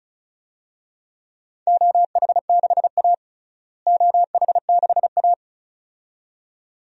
Morse code, heard as a keyed tone, about 36 wpm, 700 Hz. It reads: OH6A OH6A
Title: DXCC-100 Call Signs - 4 Characters - Repeated 2x - 35wpm